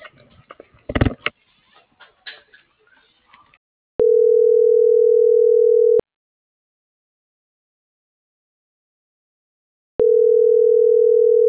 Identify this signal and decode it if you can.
Ringback tone (North America), 440 + 480 Hz.
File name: CENTRAL SAGOT / out danae.bsagot